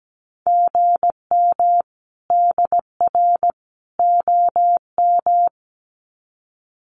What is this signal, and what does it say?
Morse code, heard as a keyed tone, about 17 wpm, 700 Hz. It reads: GM DR OM
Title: Common QSO Phrases 17wpm